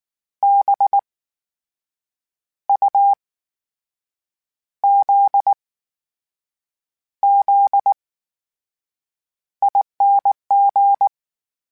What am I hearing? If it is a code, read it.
Morse code, heard as a keyed tone, about 19 wpm, spaced as for 6 wpm, 800 Hz. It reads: BUZZING
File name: Special INGs / Special INGs-mp3s / VerySlowToFaster Ing @18